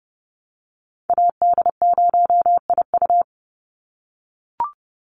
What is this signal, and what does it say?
Morse code, heard as a keyed tone, about 30 wpm, 700 Hz. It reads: AD0IU
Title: Call Signs - 30wpm